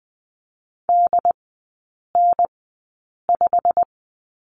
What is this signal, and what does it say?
Morse code, heard as a keyed tone, about 20 wpm, spaced as for 8 wpm, 700 Hz. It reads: DN5
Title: 152 Lesson 25e - Letter p - Three Characters 20wpm